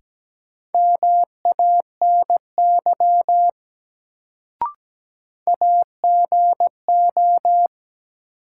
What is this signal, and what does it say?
Morse code, heard as a keyed tone, about 17 wpm, 700 Hz. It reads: MANY AGO
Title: Sets of 2 Words - New Word Drill - 17wpm